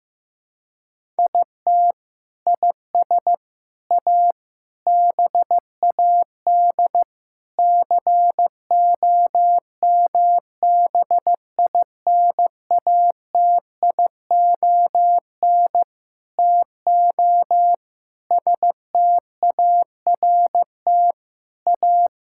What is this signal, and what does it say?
Morse code, heard as a keyed tone, about 15 wpm, 700 Hz. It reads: IT IS A BAD COMBINATION TO START A
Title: Sentences from Top 1000 Words - Review Part 6 - 15wpm